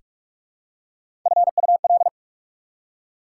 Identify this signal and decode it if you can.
Morse code, heard as a keyed tone, about 44 wpm, 700 Hz. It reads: FUL
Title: Prefixes and Suffixes 45wpm